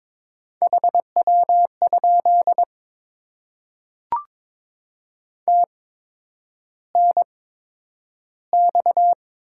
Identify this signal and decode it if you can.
Morse code, heard as a keyed tone, about 22 wpm, spaced as for 9 wpm, 700 Hz. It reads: HW? TNX
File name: Top 20 QSO Elements - 8x Character Spacing - 22wpm